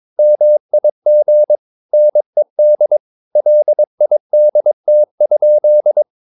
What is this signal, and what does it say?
Morse code, heard as a keyed tone, about 22 wpm, 600 Hz. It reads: MIG NED LIDT?
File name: Ferie QRS 22wpm